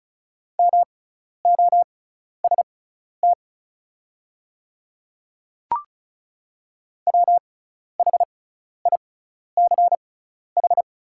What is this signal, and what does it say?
Morse code, heard as a keyed tone, about 34 wpm, spaced as for 12 wpm, 700 Hz. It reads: MOST WHICH
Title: 6x Character Spacing - Top 300 Words - 35wpm